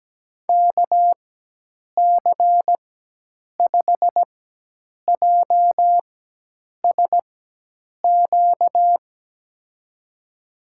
Morse code, heard as a keyed tone, about 17 wpm, spaced as for 8 wpm, 700 Hz. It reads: KC5JSQ
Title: USA Call Signs - 6 Characters - 4x Character Spacing - 17wpm